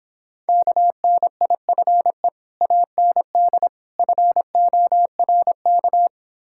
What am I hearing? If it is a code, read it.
Morse code, heard as a keyed tone, about 26 wpm, 700 Hz. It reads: KNIFE AND FORK